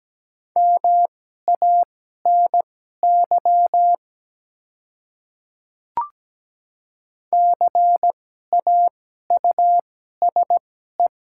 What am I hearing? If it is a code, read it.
Morse code, heard as a keyed tone, about 17 wpm, spaced as for 12 wpm, 700 Hz. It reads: MANY CAUSE
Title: 2x Character Spacing - Top 200 Words - 17wpm